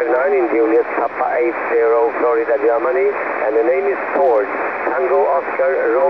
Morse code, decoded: none